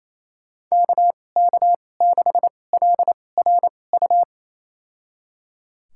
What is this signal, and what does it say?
Morse code, heard as a keyed tone, about 28 wpm, spaced as for 20 wpm, 700 Hz. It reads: KK6LRU
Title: USA Call Signs - 6 Characters - 2x Character Spacing - 28wpm